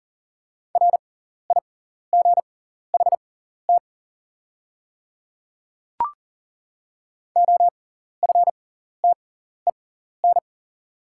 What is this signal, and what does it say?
Morse code, heard as a keyed tone, about 40 wpm, spaced as for 14 wpm, 690 Hz. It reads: RIGHT OFTEN
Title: 6x Character Spacing - Top 500 Words - 40wpm